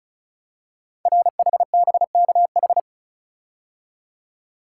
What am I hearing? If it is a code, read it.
Morse code, heard as a keyed tone, about 36 wpm, 700 Hz. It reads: RHBKH
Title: Code Groups - 5 Letters - 35wpm